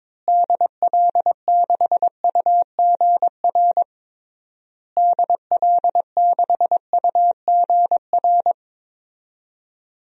Morse code, heard as a keyed tone, about 21 wpm, 700 Hz. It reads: DL6UGR DL6UGR
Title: DXCC-100 Call Signs - 6 Characters - Repeated 3x - 22wpm